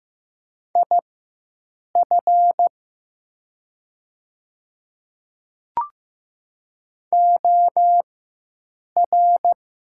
Morse code, heard as a keyed tone, about 15 wpm, spaced as for 7 wpm, 700 Hz. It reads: IF OR
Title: 4x Character Spacing - Top 100 Words - 15wpm